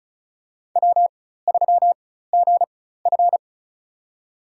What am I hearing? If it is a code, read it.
Morse code, heard as a keyed tone, about 34 wpm, spaced as for 16 wpm, 700 Hz. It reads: W3GF
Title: USA Call Signs - 4 Characters - 4x Character Spacing - 35wpm